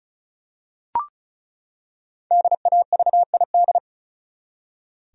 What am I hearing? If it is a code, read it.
Morse code, heard as a keyed tone, about 36 wpm, 700 Hz. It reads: DAVID
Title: USA Common Names - 35wpm